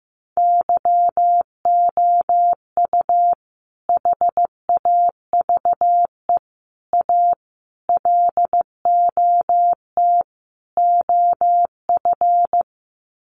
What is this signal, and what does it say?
Morse code, heard as a keyed tone, about 15 wpm, 700 Hz. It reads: YOU HAVE A LOT OF
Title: Sets of 5 Words Encore 15wpm